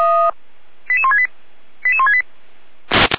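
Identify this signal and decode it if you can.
Five-tone selective call (ZVEI), heard as quick runs of five tones, tones of about 70 ms. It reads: 88168 88168
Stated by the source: ZVEI five-tone call received by SDR, from the feuerwehr-orlamuende.org fire brigade alerting audio archive.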